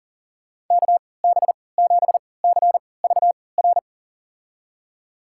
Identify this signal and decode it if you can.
Morse code, heard as a keyed tone, about 40 wpm, spaced as for 23 wpm, 700 Hz. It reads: KB7CVR
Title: USA Call Signs - 6 Characters - 3x Character Spacing - 40wpm